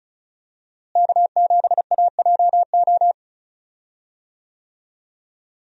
Morse code, heard as a keyed tone, about 36 wpm, 700 Hz. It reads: K7AJO
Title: USA Call Signs - 5 Characters - Repeated 3x - 35wpm